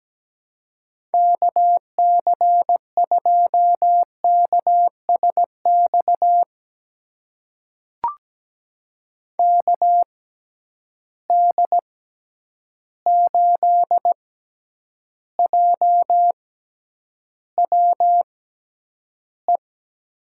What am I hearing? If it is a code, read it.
Morse code, heard as a keyed tone, about 17 wpm, spaced as for 9 wpm, 700 Hz. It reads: KC2KSX KD8JWE
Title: USA Call Signs - 6 Characters - 6x Character Spacing - 17wpm